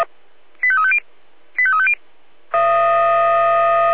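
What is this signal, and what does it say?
Five-tone selective call (ZVEI), heard as quick runs of five tones, tones of about 70 ms. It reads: 85350 85350
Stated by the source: ZVEI five-tone selective call received by SDR.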